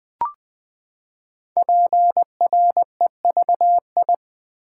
Morse code, heard as a keyed tone, about 20 wpm, 700 Hz. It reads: PREVI
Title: Build A Word - 5 Letters At A Time - 10-Letter Words - 20wpm